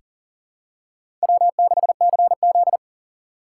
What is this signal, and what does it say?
Morse code, heard as a keyed tone, about 40 wpm, 700 Hz. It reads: W6CZ
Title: USA Call Signs - 4 Characters - 4x Character Spacing - 40wpm